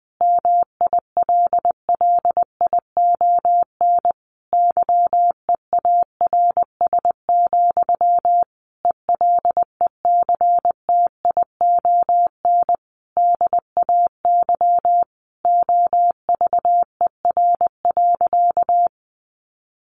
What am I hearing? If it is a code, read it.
Morse code, heard as a keyed tone, about 20 wpm, 700 Hz. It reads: MILLION YEARS, ELECTION DAY OVER.
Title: A Sound Of Thunder 20wpm